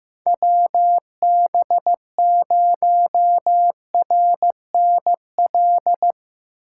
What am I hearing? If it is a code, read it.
Morse code, heard as a keyed tone, about 15 wpm, 700 Hz. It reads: WB0RNL
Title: USA Call Signs - 6 Characters - 6x Character Spacing - 15wpm